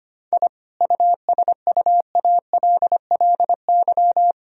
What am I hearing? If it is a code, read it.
Morse code, heard as a keyed tone, about 25 wpm, 700 Hz. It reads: I USUALLY
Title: Sentences from Top 1000 Words - Review Part 3 - 25wpm